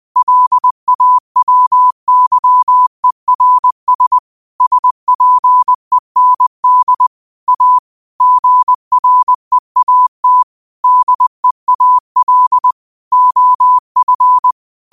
Morse code, heard as a keyed tone, about 20 wpm, 1000 Hz. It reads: LAWYERS SPEND A GREAT DEAL OF
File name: qotdfiles / 20wpm